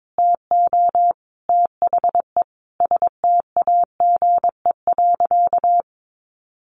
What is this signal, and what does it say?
Morse code, heard as a keyed tone, about 22 wpm, 700 Hz. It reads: TO THE STAGE.
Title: Sentences from Top 1500 Words 22wpm